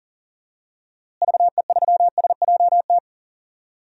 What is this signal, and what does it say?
Morse code, heard as a keyed tone, about 40 wpm, 700 Hz. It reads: VE3SJT
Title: DXCC-100 Call Signs - 6 Characters - Repeated 3x - 40wpm